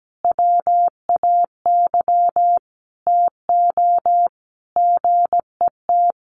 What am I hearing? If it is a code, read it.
Morse code, heard as a keyed tone, about 17 wpm, 700 Hz. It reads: WAY TO GET
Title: Sets of 5 Words Encore2 17wpm